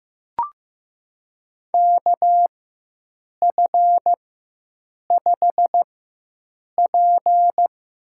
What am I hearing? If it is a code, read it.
Morse code, heard as a keyed tone, about 15 wpm, spaced as for 7 wpm, 700 Hz. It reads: KF5P
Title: USA Call Signs - 4x Character Spacing - 15wpm